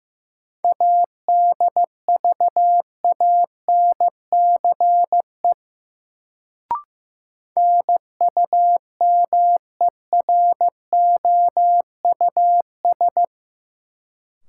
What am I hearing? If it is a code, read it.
Morse code, heard as a keyed tone, about 15 wpm, 700 Hz. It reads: ADVANCE NUMEROUS
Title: Sentences from Top 1400 Words - New Word Drill Alt - 15wpm